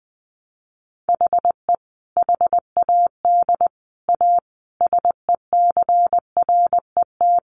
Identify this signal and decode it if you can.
Morse code, heard as a keyed tone, about 20 wpm, 700 Hz. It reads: HE HAD A SECRET
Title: Sentences from Top 1200 Words 20wpm